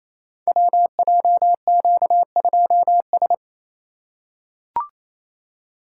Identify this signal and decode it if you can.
Morse code, heard as a keyed tone, about 28 wpm, 700 Hz. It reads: WJQ2S
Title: Code Groups - 5 Letter:Numbers - 28wpm